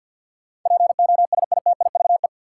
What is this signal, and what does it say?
Morse code, heard as a keyed tone, about 51 wpm, 700 Hz. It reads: POSITIVE
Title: Sentences from Top 1400 Words - New Word Drill Alt - 50wpm